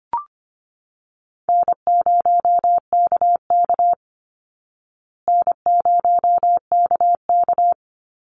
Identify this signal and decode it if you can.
Morse code, heard as a keyed tone, about 25 wpm, 700 Hz. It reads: N0KK N0KK
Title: Easy Call Signs Repeated 3x - 25wpm